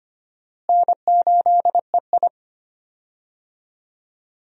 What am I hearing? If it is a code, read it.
Morse code, heard as a keyed tone, about 25 wpm, 700 Hz. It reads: N8EI
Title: USA Call Signs - 4 Characters - Repeated 2x - 25wpm